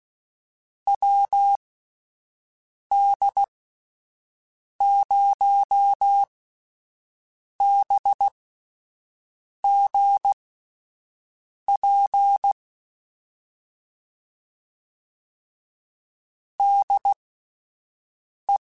Morse code, heard as a keyed tone, about 16 wpm, spaced as for 5 wpm, 800 Hz. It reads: WD0BGP DE
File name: QSOs / QSOs-mp3s / WD0BGP 1@13